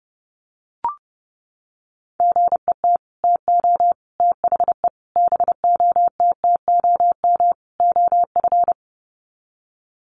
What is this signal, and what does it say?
Morse code, heard as a keyed tone, about 30 wpm, 700 Hz. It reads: GET TO THE BOTTOM OF
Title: Sets of 5 Words Encore 30wpm